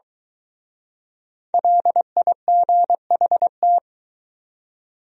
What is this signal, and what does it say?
Morse code, heard as a keyed tone, about 23 wpm, 700 Hz. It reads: LIGHT